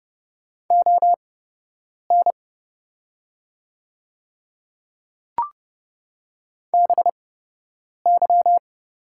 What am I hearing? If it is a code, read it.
Morse code, heard as a keyed tone, about 30 wpm, spaced as for 8 wpm, 700 Hz. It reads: ON BY
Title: 8x Character Spacing - Top 100 Words - 30wpm